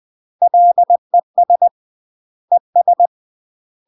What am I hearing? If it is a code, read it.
Morse code, heard as a keyed tone, about 20 wpm, 700 Hz. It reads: LES ES
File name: UK MASTER QSOs TRAINING / 20WPM / UK MASTER QSOs 0063